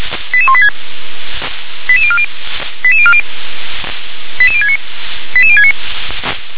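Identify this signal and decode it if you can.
Five-tone selective call (ZVEI), heard as quick runs of five tones, tones of about 70 ms. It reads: 88186 88040 88040 88069 88069
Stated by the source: ZVEI five-tone call 88168 received by SDR.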